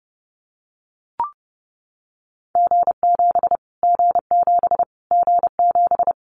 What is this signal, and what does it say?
Morse code, heard as a keyed tone, about 30 wpm, 690 Hz. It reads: G7 G7 G7
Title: Callsign Prefixes for Top 25 DXCC - Repeated 3x - 30wpm